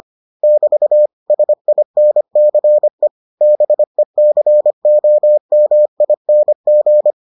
Morse code, heard as a keyed tone, about 25 wpm, 600 Hz. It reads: <BT> SINCE BECOMING